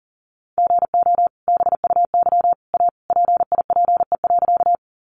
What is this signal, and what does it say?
Morse code, heard as a keyed tone, about 40 wpm, 700 Hz. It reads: GO BUY A PIPE.
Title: Sentences from Top 1500 Words 40wpm